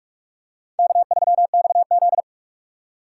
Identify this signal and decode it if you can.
Morse code, heard as a keyed tone, about 45 wpm, 690 Hz. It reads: K3XZ
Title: USA Call Signs - 4 Characters - 2x Character Spacing - 45wpm